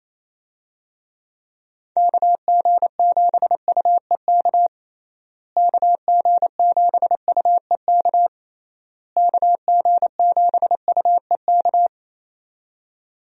Morse code, heard as a keyed tone, about 28 wpm, 700 Hz. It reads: KG7UEK KG7UEK KG7UEK
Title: USA Call Signs - 6 Characters - Repeated 3x - 28wpm